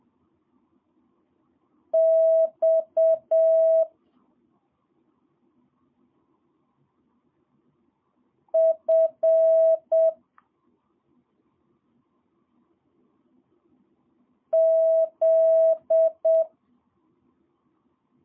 Morse code, heard as a keyed tone, about 7 wpm, 650 Hz. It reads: X F Z
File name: jh8cbh / 2022cw